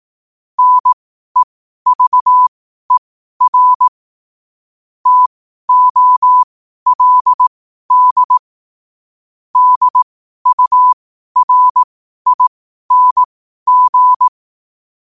Morse code, heard as a keyed tone, about 18 wpm, spaced as for 12 wpm, 1000 Hz. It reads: NEVER TOLD DURING
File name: 12wpm